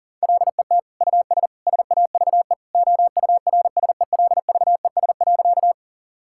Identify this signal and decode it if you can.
Morse code, heard as a keyed tone, about 40 wpm, 700 Hz. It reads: LET US SAVE OURSELVES.
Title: Sentences from Top 1600 Words 40wpm